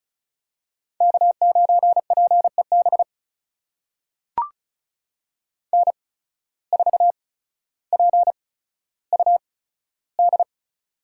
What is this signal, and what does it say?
Morse code, heard as a keyed tone, about 36 wpm, spaced as for 15 wpm, 700 Hz. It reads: K9PEB N4PUD
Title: USA Call Signs - 8x Character Spacing - 35wpm